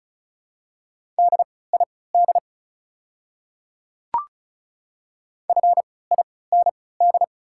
Morse code, heard as a keyed tone, about 36 wpm, spaced as for 20 wpm, 690 Hz. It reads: DID FIND
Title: 3x Character Spacing - Top 100 Words - 35wpm